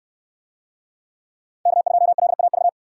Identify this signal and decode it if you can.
Morse code, heard as a keyed tone, about 62 wpm, 700 Hz. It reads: K2LAV